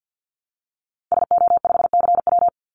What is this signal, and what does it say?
Morse code, heard as a keyed tone, about 50 wpm, 700 Hz. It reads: SO5CW
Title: Call Signs - 50wpm